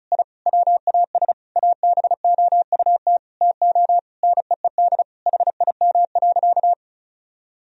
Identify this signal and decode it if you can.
Morse code, heard as a keyed tone, about 36 wpm, 700 Hz. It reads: I WAS ABOUT TO NEED HIM.
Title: Sentences from Top 1000 Words - Review Part 1 - 35wpm